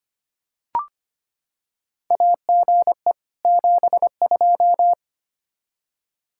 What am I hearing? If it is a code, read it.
Morse code, heard as a keyed tone, about 25 wpm, 700 Hz. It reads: AGE 72
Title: Practice QSO - Exchange 3 - More Info - 25wpm